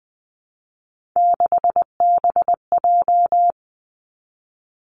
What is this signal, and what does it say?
Morse code, heard as a keyed tone, about 20 wpm, 700 Hz. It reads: 6BJ
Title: Three Challenging Letter-Number Combinations 20wpm